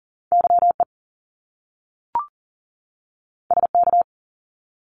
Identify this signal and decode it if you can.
Morse code, heard as a keyed tone, about 40 wpm, 700 Hz. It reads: YE SK
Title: Top Two Letter Combinations in Words 40wpm